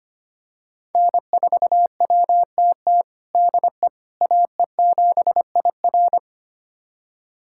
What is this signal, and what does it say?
Morse code, heard as a keyed tone, about 25 wpm, 700 Hz. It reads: N4WTT DE AE7IR
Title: USA Callsign DE Callsign - 5 Characters - 25wpm